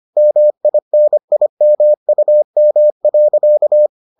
Morse code, heard as a keyed tone, about 25 wpm, 600 Hz. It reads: MINIMUM.